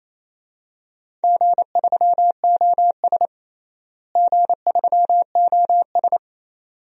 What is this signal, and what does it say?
Morse code, heard as a keyed tone, about 28 wpm, 700 Hz. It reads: G3OS G3OS
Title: DXCC-100 Call Signs - 4 Characters - Repeated 2x - 28wpm